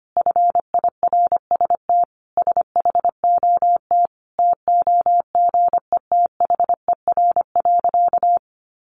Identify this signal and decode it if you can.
Morse code, heard as a keyed tone, about 25 wpm, 700 Hz. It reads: FIRST SHOT TOGETHER.
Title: Sentences from Top 1100 Words 25wpm